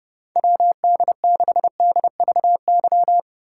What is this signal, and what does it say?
Morse code, heard as a keyed tone, about 38 wpm, 700 Hz. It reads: WD6DVY